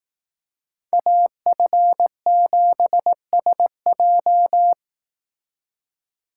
Morse code, heard as a keyed tone, about 18 wpm, 700 Hz. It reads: AF7SJ
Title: Speed Racing - USA 2x2 Call Signs 15wpm